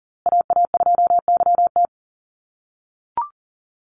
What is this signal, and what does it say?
Morse code, heard as a keyed tone, about 40 wpm, 700 Hz. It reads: AA2YT
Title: USA Call Signs Repeated 3x 40wpm